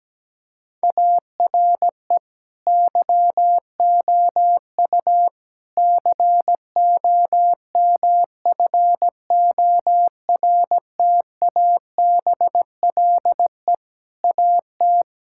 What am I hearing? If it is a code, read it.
Morse code, heard as a keyed tone, about 17 wpm, 700 Hz. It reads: ARE YOU COMFORTABLE AT
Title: Sentences from Top 1600 Words 17wpm